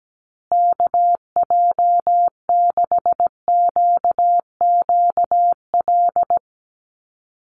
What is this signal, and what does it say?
Morse code, heard as a keyed tone, about 17 wpm, 700 Hz. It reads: KJ6QQL
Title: USA Call Signs Repeated 3x 17wpm